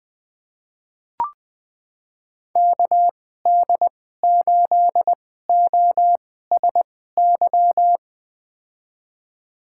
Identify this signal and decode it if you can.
Morse code, heard as a keyed tone, about 20 wpm, spaced as for 14 wpm, 700 Hz. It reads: KD8OSY
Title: USA Call Signs - 6 Characters - 2x Character Spacing - 20wpm